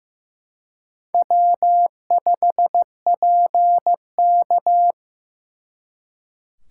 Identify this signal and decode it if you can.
Morse code, heard as a keyed tone, about 18 wpm, 700 Hz. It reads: W5PK
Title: Speed Racing - USA 1x2 Call Signs 15wpm